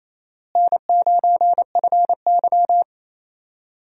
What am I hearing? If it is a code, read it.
Morse code, heard as a keyed tone, about 28 wpm, 700 Hz. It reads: N9FY
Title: USA Call Signs - 4 Characters - Repeated 2x - 28wpm